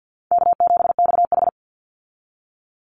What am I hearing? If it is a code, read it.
Morse code, heard as a keyed tone, about 50 wpm, 700 Hz. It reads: K7XH
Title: Call Signs Repeated 3x - 50wpm